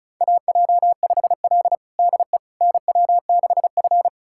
Morse code, heard as a keyed tone, about 36 wpm, 700 Hz. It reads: AJ5L DE NW6F
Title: USA Callsign DE Callsign - 4 Characters - 35wpm